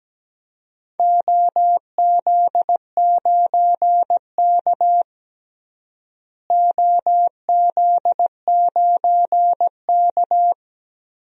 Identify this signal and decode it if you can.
Morse code, heard as a keyed tone, about 17 wpm, 700 Hz. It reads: OZ9K OZ9K
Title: DXCC-100 Call Signs - 4 Characters - Repeated 3x - 17wpm